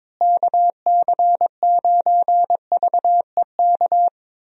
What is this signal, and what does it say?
Morse code, heard as a keyed tone, about 22 wpm, 700 Hz. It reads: KC9VEK
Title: USA Call Signs - 6 Characters - Repeated 3x - 22wpm